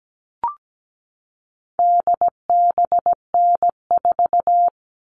Morse code, heard as a keyed tone, about 17 wpm, 700 Hz. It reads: DBN4
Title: All Cut Numbers - 4-Digit 17wpm